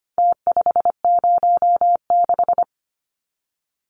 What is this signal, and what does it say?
Morse code, heard as a keyed tone, about 25 wpm, 700 Hz. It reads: T506
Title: Numbers - Contest Style - 4-Digit 25wpm